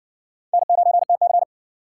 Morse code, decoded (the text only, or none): N9TC